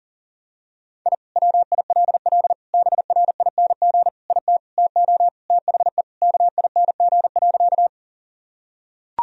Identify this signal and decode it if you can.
Morse code, heard as a keyed tone, about 40 wpm, 700 Hz. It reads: I WILL BRING IT TO THE KING.
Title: Sentences from Top 1000 Words - Review Part 3 - 40wpm